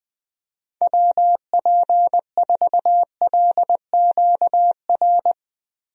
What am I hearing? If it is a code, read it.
Morse code, heard as a keyed tone, about 20 wpm, 700 Hz. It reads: WP4LQR
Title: USA Call Signs - 6 Characters - Repeated 3x - 20wpm